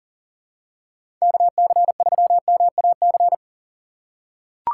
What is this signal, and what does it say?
Morse code, heard as a keyed tone, about 40 wpm, 700 Hz. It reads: KC3MAC
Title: USA Call Signs - 6 Characters - 2x Character Spacing - 40wpm